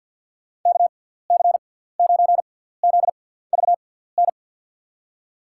Morse code, heard as a keyed tone, about 51 wpm, spaced as for 17 wpm, 700 Hz. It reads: KC9ZVN
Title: USA Call Signs - 6x Character Spacing - 50wpm